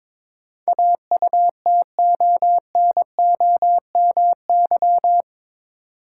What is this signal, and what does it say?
Morse code, heard as a keyed tone, about 22 wpm, 700 Hz. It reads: AUTONOMY
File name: practice / 22wpm / Eight Letter Words - 22wpm